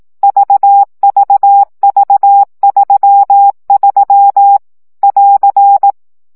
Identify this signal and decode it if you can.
Morse code, heard as a keyed tone, about 18 wpm, 800 Hz. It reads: VVV33 <AR>
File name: CW gyakorlas v 3 15 baud